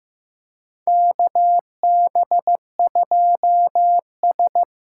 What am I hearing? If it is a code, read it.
Morse code, heard as a keyed tone, about 15 wpm, 700 Hz. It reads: KB2S
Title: K1USN SST QSO Practice - 15wpm